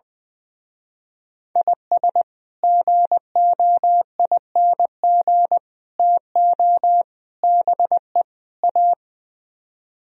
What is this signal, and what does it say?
Morse code, heard as a keyed tone, about 20 wpm, 700 Hz. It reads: IS GOING TO BE A
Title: Speed-Racing - Sets of 5 Words 20wpm